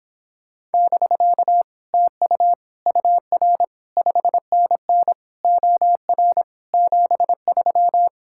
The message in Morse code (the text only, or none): <BK> TU UR 5NN OR 73